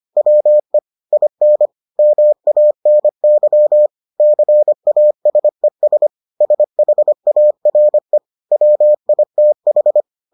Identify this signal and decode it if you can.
Morse code, heard as a keyed tone, about 25 wpm, 600 Hz. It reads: WE IN MANY CASES SHARE WITH